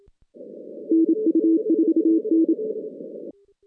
Morse code, heard as a keyed tone, about 27 wpm, 320 Hz. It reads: NU4N
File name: 2016 JAN CW NAQP / 2016 Jan naqp k5ot - 0-10